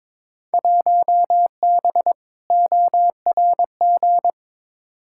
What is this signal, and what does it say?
Morse code, heard as a keyed tone, about 22 wpm, 700 Hz. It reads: 1B ORG
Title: Field Day Exchanges Encore - 22wpm